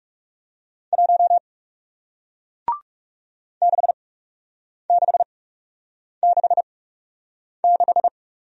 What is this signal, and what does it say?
Morse code, heard as a keyed tone, about 40 wpm, spaced as for 8 wpm, 700 Hz. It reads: 1 6666